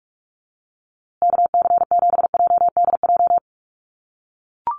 Morse code, heard as a keyed tone, about 44 wpm, 700 Hz. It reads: KC7JDJ
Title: USA Call Signs Repeated 2x 45wpm